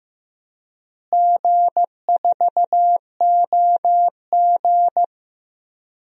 Morse code, heard as a keyed tone, about 15 wpm, 700 Hz. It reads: G4OG
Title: DXCC-100 Call Signs - 4 Characters - Repeated 3x - 15wpm